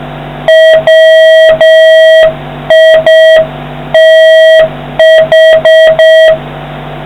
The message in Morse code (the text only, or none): WITH